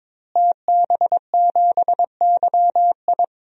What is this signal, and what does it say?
Morse code, heard as a keyed tone, about 22 wpm, 700 Hz. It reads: TB7YI
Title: DXCC-100 Call Signs - 5 Characters - Repeated 2x - 22wpm